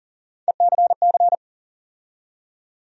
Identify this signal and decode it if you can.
Morse code, heard as a keyed tone, about 40 wpm, 700 Hz. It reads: ECC
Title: USA Call Sign Postfixes - 3 Letters 40wpm